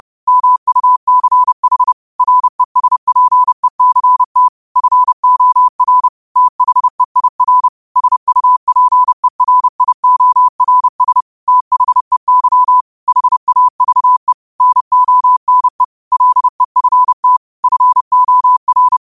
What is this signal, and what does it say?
Morse code, heard as a keyed tone, about 30 wpm, 1000 Hz. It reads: MUCH RESPECT FOR THEIR SUPERIORS THEY HAVE NONE LEFT FOR
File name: 30wpm